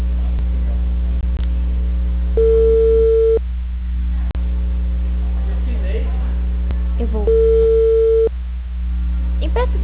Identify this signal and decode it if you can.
Ringback tone (China, 450 Hz), with speech.
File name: out- -2010- - - .18720